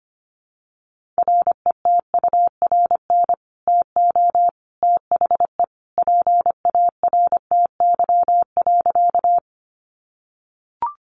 Morse code, heard as a keyed tone, about 25 wpm, 700 Hz. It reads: RETURN TO THE PARTY.
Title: Sentences from Top 1400 Words 25wpm